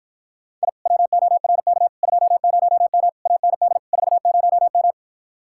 Morse code, heard as a keyed tone, about 52 wpm, 700 Hz. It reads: I WORK 20M AND 40M